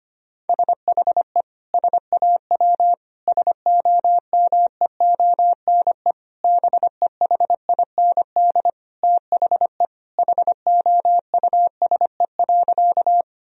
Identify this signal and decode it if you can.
Morse code, heard as a keyed tone, about 25 wpm, 700 Hz. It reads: SHE SAW SOMEONE BEHIND THE HOUSE.
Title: Sentences from Top 1000 Words - Review Part 4 - 25wpm